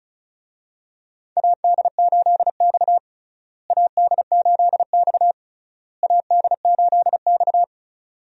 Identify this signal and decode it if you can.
Morse code, heard as a keyed tone, about 34 wpm, 700 Hz. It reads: AD8X AD8X AD8X
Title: USA Call Signs - 4 Characters - Repeated 3x - 35wpm